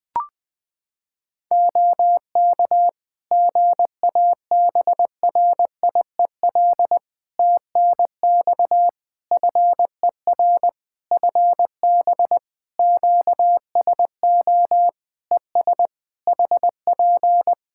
Morse code, heard as a keyed tone, about 20 wpm, 700 Hz. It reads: OK GABRIEL TNX FER FB QSO ES HP
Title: Practice QSO - Exchange 4 - Ending - 20wpm